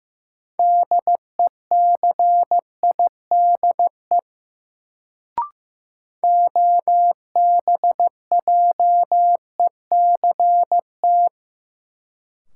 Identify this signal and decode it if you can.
Morse code, heard as a keyed tone, about 15 wpm, 700 Hz. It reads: DECIDE OBJECT
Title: Sentences from Top 500 Words - New Word Drill - 15wpm